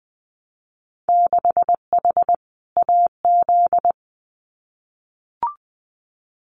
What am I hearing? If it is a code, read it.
Morse code, heard as a keyed tone, about 20 wpm, 700 Hz. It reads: 6H AZ
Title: Winter Field Day Exchanges Encore 20wpm